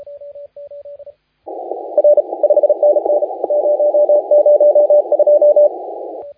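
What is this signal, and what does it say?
Morse code, heard as a keyed tone, about 36 wpm, 590 Hz. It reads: R 5NN 002
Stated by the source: Morse contest audio recorded by LA4C during the SAC CW 2018 contest.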